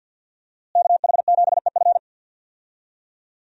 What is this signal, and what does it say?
Morse code, heard as a keyed tone, about 48 wpm, 700 Hz. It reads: KH7EF
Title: USA Call Signs - 5 Characters - Repeated 2x - 50wpm